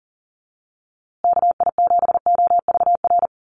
Morse code, heard as a keyed tone, about 40 wpm, 700 Hz. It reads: KI7OVR
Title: USA Call Signs Repeated 3x 40wpm